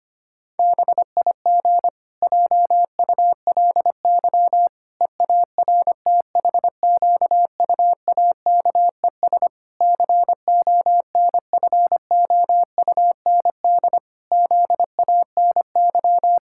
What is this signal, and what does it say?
Morse code, heard as a keyed tone, about 25 wpm, 700 Hz. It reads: BIG JULY EARTHQUAKES CONFOUND ZANY
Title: Pangram Challenge v2 - 25wpm